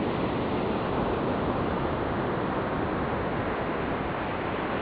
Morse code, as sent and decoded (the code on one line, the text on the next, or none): none